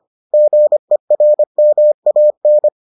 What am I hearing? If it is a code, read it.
Morse code, heard as a keyed tone, about 25 wpm, 600 Hz. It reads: GERMAN